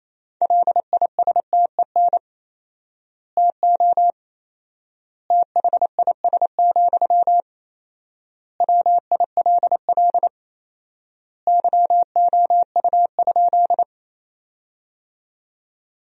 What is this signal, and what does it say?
Morse code, heard as a keyed tone, about 28 wpm, 700 Hz. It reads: LISTEN TO THIS, WILL YOU?
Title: 4x Word-Spacing - Sentences from Top 400 Words - 28wpm